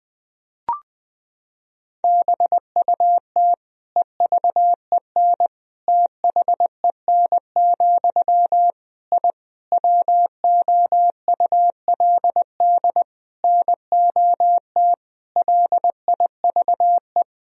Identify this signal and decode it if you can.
Morse code, heard as a keyed tone, about 20 wpm, 700 Hz. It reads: BUT EVEN THEN, I WOULD NOT LIVE
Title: Sentences from Top 1000 Words - Review Part 1 - 20wpm